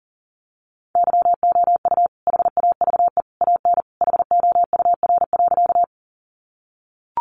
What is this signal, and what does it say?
Morse code, heard as a keyed tone, about 40 wpm, 700 Hz. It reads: YOU HAVE AN HOUR.
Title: Sentences from Top 400 Words 40wpm